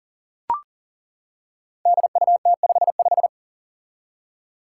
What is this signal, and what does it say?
Morse code, heard as a keyed tone, about 40 wpm, 700 Hz. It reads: DUT55
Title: Five Challenging Letter-Number Combinations 40wpm